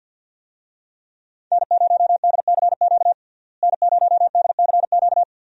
Morse code, heard as a keyed tone, about 52 wpm, 690 Hz. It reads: N0DCQ N0DCQ
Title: USA Call Signs - 5 Characters - Repeated 3x - 50wpm